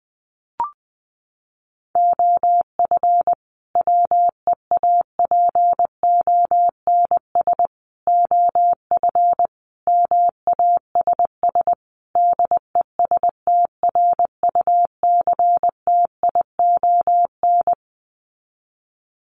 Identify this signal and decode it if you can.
Morse code, heard as a keyed tone, about 20 wpm, 700 Hz. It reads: OF WEAPONS OF MASS DESTRUCTION
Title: Sets of 5 Words Encore 20wpm